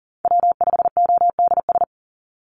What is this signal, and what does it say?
Morse code, heard as a keyed tone, about 42 wpm, 700 Hz. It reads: W5ODS
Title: NA SOTA Activator Callsigns 40wpm